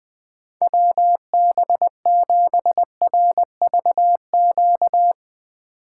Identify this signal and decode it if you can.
Morse code, heard as a keyed tone, about 20 wpm, 700 Hz. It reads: WB7RVQ
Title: USA Call Signs - 6 Characters - 6x Character Spacing - 20wpm